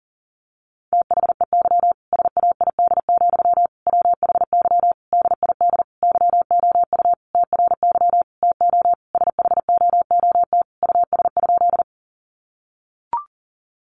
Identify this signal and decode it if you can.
Morse code, heard as a keyed tone, about 40 wpm, 700 Hz. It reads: THEY SAID, WHY DID YOU TRY TO SHOOT US?
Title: Sentences from Top 1200 Words 40wpm